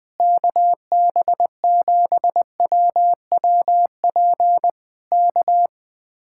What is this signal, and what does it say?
Morse code, heard as a keyed tone, about 20 wpm, 700 Hz. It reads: KB7WWP K
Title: Speed-Racing Common QSO Phrases - 20wpm